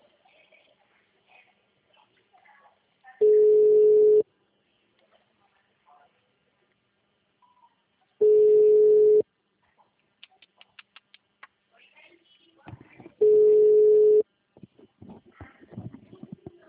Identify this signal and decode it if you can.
Ringback tone (Europe, 425 Hz), with speech.
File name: out- -2019- - - .16860